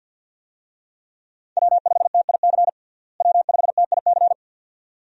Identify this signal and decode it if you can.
Morse code, heard as a keyed tone, about 50 wpm, 700 Hz. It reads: W5TIC W5TIC
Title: USA Call Signs - 5 Characters - Repeated 2x - 50wpm